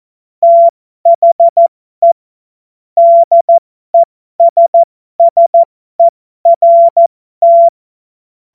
Morse code, heard as a keyed tone, about 14 wpm, spaced as for 12 wpm, 680 Hz. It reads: THE DESSERT
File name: Morsekurs / Witze